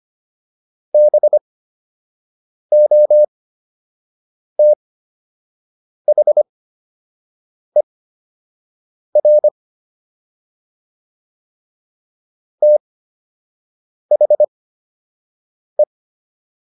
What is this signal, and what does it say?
Morse code, heard as a keyed tone, about 25 wpm, spaced as for 6 wpm, 600 Hz. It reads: BOTHER THE